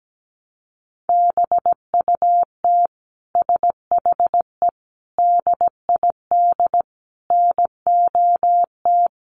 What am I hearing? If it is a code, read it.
Morse code, heard as a keyed tone, about 17 wpm, 700 Hz. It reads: BUT SHE DID NOT
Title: Sentences from Top 600 Words 17wpm